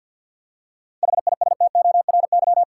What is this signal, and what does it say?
Morse code, heard as a keyed tone, about 50 wpm, 700 Hz. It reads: HISTORY
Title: Seven Letter Words - 50wpm